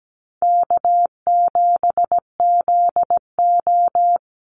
Morse code, heard as a keyed tone, about 17 wpm, 700 Hz. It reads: K7ZO
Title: NA SOTA Chaser Callsigns 17wpm